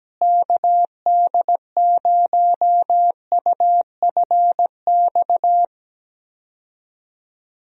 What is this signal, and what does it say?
Morse code, heard as a keyed tone, about 17 wpm, 700 Hz. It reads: KD0UFX
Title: USA Call Signs - 6 Characters - Repeated 2x - 17wpm